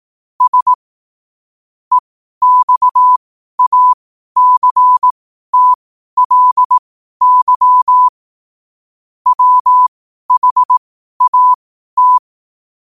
Morse code, heard as a keyed tone, about 18 wpm, spaced as for 12 wpm, 1000 Hz. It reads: S EXACTLY WHAT
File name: qotdfiles / 12wpm